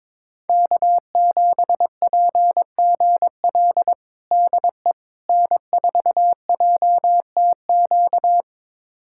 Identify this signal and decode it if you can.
Morse code, heard as a keyed tone, about 22 wpm, 700 Hz. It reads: K7PGL DE N4JTQ
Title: USA Callsign DE Callsign - 5 Characters - 22wpm